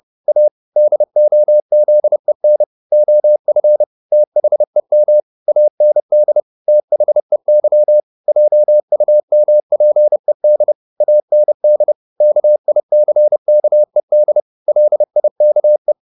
Morse code, heard as a keyed tone, about 30 wpm, 600 Hz. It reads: A DOZEN OF THEM AND THEY JUMPED AND KICKED LIKE